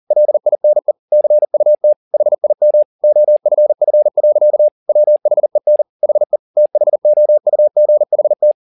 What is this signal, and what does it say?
Morse code, heard as a keyed tone, about 40 wpm, 600 Hz. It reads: LINE CUT HIM OFF. WHEN HE THOUGHT
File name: the old man and the sea 2 40wpm